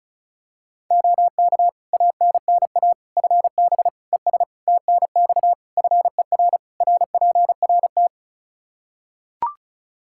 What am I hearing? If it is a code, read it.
Morse code, heard as a keyed tone, about 34 wpm, 700 Hz. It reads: OK ANNA FB ES TNX FER RPRT
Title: Practice QSO - Exchange 2 - Info - 35wpm